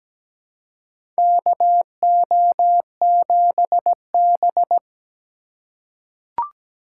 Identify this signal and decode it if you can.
Morse code, heard as a keyed tone, about 17 wpm, 700 Hz. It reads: KO7B
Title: USA Call Signs - 4 Characters - 2x Character Spacing - 17wpm